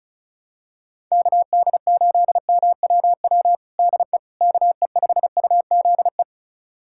Morse code, heard as a keyed tone, about 35 wpm, 700 Hz. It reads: KD8MWW DE KE5UZE
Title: USA Callsign DE Callsign - 6 Characters - 35wpm